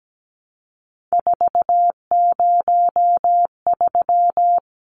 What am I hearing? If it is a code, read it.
Morse code, heard as a keyed tone, about 17 wpm, 700 Hz. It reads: 403
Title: Numbers - 3-Digit 17wpm